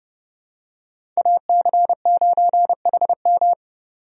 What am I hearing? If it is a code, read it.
Morse code, heard as a keyed tone, about 29 wpm, 700 Hz. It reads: AC9HM
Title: USA Call Signs - 5 Characters - 30wpm